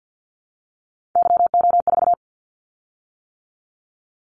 Morse code, heard as a keyed tone, about 50 wpm, 700 Hz. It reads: YO4
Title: Callsign Prefixes for Top 25 DXCC - 50wpm